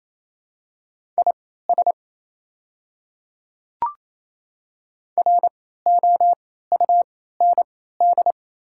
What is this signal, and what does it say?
Morse code, heard as a keyed tone, about 28 wpm, spaced as for 16 wpm, 700 Hz. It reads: IS ROUND
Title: 3x Character Spacing - Top 300 Words - 28wpm